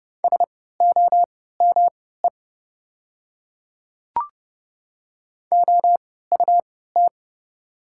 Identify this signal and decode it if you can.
Morse code, heard as a keyed tone, about 30 wpm, spaced as for 17 wpm, 700 Hz. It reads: SOME OUT
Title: 3x Character Spacing - Top 100 Words - 30wpm